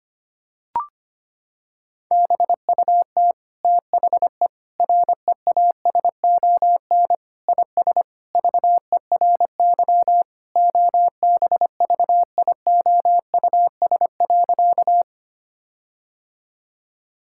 Morse code, heard as a keyed tone, about 25 wpm, 700 Hz. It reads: BUT THE REASON IS VERY OBVIOUS.
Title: Sentences from Top 1600 Words 25wpm